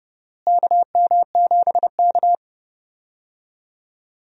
Morse code, heard as a keyed tone, about 32 wpm, 700 Hz. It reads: KM7K
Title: USA Call Signs - 4 Characters - Repeated 3x - 30wpm